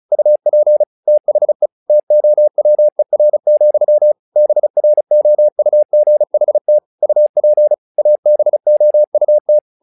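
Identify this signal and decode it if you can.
Morse code, heard as a keyed tone, about 35 wpm, 600 Hz. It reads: UP THE TOWER, BROUGHT UP ABOUT
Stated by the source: Morse practice text "Tower Trouble" sent at 35 WPM, from the SKCC - Straight Key Century Club learning center audio files.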